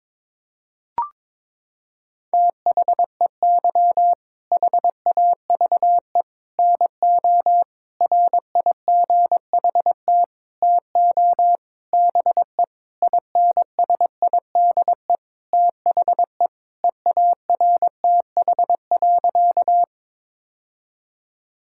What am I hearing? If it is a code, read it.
Morse code, heard as a keyed tone, about 22 wpm, 700 Hz. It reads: THEY HAVE NO RIGHT TO BE INSIDE THE EARTH.
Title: Sentences from Top 1000 Words - Review Part 6 - 22wpm